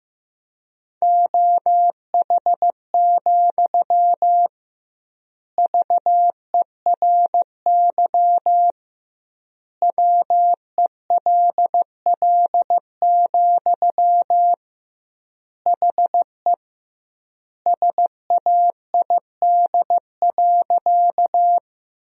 Morse code, heard as a keyed tone, about 16 wpm, 700 Hz. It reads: OH, VERY WELL, HE SAID.
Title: 2x Word-Spacing - Sentences from Top 500 Words - 15wpm